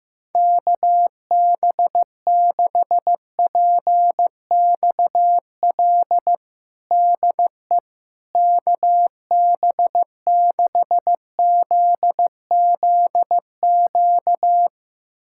Morse code, heard as a keyed tone, about 15 wpm, 700 Hz. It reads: KB6PXL DE KB6ZZQ
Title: USA Callsign DE Callsign - 6 Characters - 15wpm